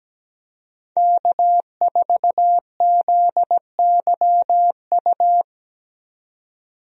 Morse code, heard as a keyed tone, about 17 wpm, 700 Hz. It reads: K4ZYU
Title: USA Call Signs - 5 Characters - 17wpm